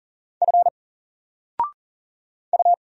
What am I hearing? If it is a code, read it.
Morse code, heard as a keyed tone, about 39 wpm, 700 Hz. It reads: F U
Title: Single Letter-Number 40wpm